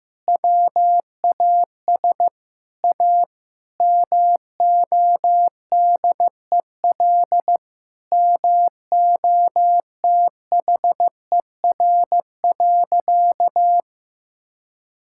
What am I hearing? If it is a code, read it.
Morse code, heard as a keyed tone, about 15 wpm, 700 Hz. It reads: WAS A MODEL MOTHER.
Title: Sentences from Top 1000 Words - Review Part 5 - 15wpm